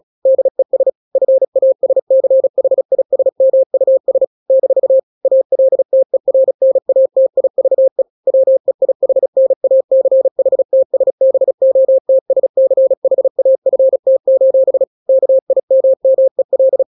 Morse code, heard as a keyed tone, about 36 wpm, 530 Hz. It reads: DES FASCHISMUS - ALTERNATIVE WEIHNACHTSBOTSCHAFT: KIMMEL